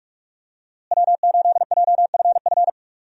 Morse code, heard as a keyed tone, about 45 wpm, 700 Hz. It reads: W8JFF
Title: USA Call Signs - 5 Characters - Repeated 3x - 45wpm